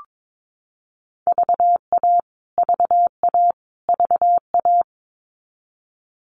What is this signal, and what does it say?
Morse code, heard as a keyed tone, about 22 wpm, 700 Hz. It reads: VA VA VA
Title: US States Repeated 3x 22wpm